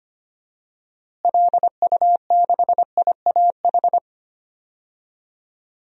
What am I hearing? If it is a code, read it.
Morse code, heard as a keyed tone, about 25 wpm, 700 Hz. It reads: LU6IAH
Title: DXCC-100 Call Signs - 6 Characters - Repeated 2x - 25wpm